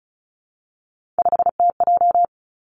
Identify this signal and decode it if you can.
Morse code, heard as a keyed tone, about 36 wpm, 700 Hz. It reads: 5TJ
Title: Three Challenging Letter-Number Combinations 35wpm